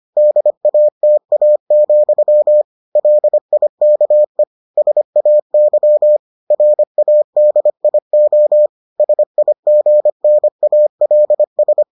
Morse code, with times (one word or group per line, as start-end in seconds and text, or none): 0.2-2.6 DATA,
3.0-4.4 LIKE
4.8-6.2 SAY
6.5-8.7 RADIO
9.0-11.8 SIGNALS